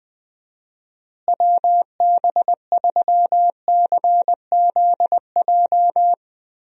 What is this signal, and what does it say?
Morse code, heard as a keyed tone, about 20 wpm, 700 Hz. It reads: WB3CZJ
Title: USA Call Signs - 6 Characters - Repeated 2x - 20wpm